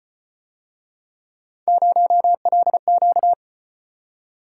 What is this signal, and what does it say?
Morse code, heard as a keyed tone, about 35 wpm, 700 Hz. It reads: 0LQ